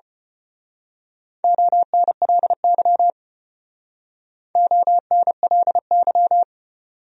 Morse code, heard as a keyed tone, about 29 wpm, 700 Hz. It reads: ONLY ONLY
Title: Speed-Racing Top 300 Words 25wpm